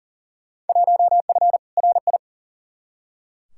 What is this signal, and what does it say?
Morse code, heard as a keyed tone, about 37 wpm, 700 Hz. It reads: 1F RI
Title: Field Day Exchanges Encore - 40wpm